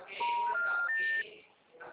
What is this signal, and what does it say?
Special information tone, with noise.
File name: out- -2005- - - .7120